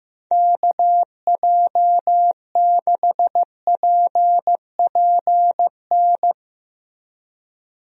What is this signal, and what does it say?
Morse code, heard as a keyed tone, about 15 wpm, 700 Hz. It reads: KJ6PPN
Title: USA Call Signs - 6 Characters - Repeated 2x - 15wpm